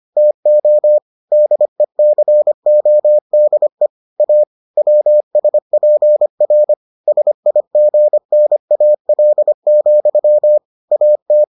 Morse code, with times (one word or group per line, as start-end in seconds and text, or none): 0.2-1.0 TO
1.3-3.9 DECODE
4.2-4.4 A
4.8-6.7 WSPR
7.1-10.6 SIGNAL,
10.9-11.4 AT